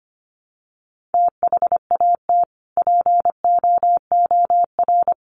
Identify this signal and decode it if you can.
Morse code, heard as a keyed tone, about 25 wpm, 690 Hz. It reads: THAT POOR
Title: Sentences from Top 1100 Words 25wpm